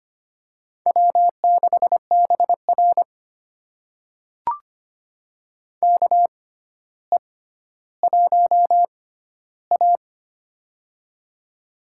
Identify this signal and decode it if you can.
Morse code, heard as a keyed tone, about 25 wpm, spaced as for 13 wpm, 700 Hz. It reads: W6BR KE1A
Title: USA Call Signs - 4 Characters - 6x Character Spacing - 25wpm